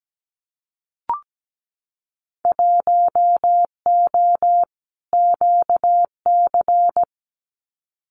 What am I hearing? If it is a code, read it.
Morse code, heard as a keyed tone, about 17 wpm, 700 Hz. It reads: 1O QC
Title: Winter Field Day Exchanges 17wpm